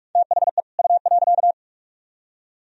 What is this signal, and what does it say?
Morse code, heard as a keyed tone, about 55 wpm, 700 Hz. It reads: THE U.
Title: Speed-Racing - Sets of 2 Words 45wpm